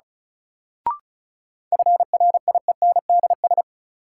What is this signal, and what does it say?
Morse code, heard as a keyed tone, about 36 wpm, 700 Hz. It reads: FRIENDS
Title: Sentences from Top 1600 Words - New Word Drill Alt - 35wpm